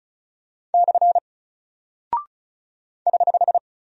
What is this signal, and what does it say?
Morse code, heard as a keyed tone, about 36 wpm, 700 Hz. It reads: / <HH>
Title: Single Character (Letter:Number:Punctuation:Prosign) Alt 35wpm